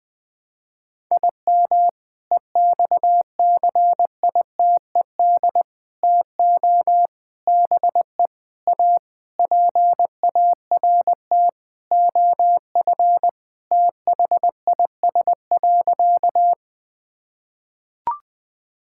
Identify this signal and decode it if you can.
Morse code, heard as a keyed tone, about 20 wpm, 700 Hz. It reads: IM EXCITED TO BE A PART OF THIS.
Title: Sentences with 5-Word Phrases - Spoken Beginning - 20wpm